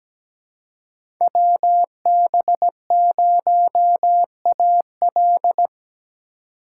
Morse code, heard as a keyed tone, about 17 wpm, 700 Hz. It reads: WB0AL
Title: USA Call Signs - 5 Characters - 2x Character Spacing - 17wpm